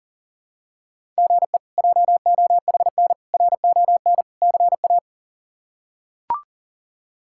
Morse code, heard as a keyed tone, about 40 wpm, 700 Hz. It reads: GE JOHN RON CA
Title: K1USN SST QSO Practice - 40wpm